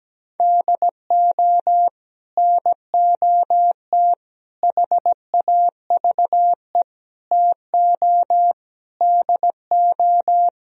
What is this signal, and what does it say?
Morse code, heard as a keyed tone, about 17 wpm, 700 Hz. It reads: DO NOT HAVE TO DO
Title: Speed-Racing - Sets of 5 Words 17wpm